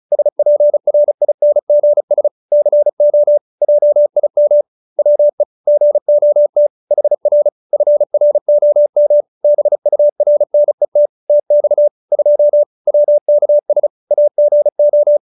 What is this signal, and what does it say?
Morse code, heard as a keyed tone, about 34 wpm, 590 Hz. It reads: SPRINGS CO JIM WE GOT HR FROM BURNET TX 2 WKS AGO
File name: QSO312 35